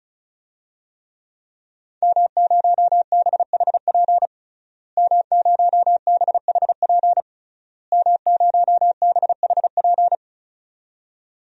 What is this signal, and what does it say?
Morse code, heard as a keyed tone, about 36 wpm, 700 Hz. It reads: M0BHP M0BHP M0BHP
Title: DXCC-100 Call Signs - 5 Characters - Repeated 3x - 35wpm